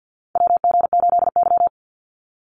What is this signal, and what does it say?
Morse code, heard as a keyed tone, about 50 wpm, 700 Hz. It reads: WG8Y
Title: NA SOTA Chaser Callsigns 50wpm